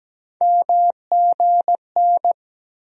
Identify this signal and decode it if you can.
Morse code, heard as a keyed tone, about 17 wpm, 700 Hz. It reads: MGN
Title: USA Call Sign Postfixes - 3 Letters 17wpm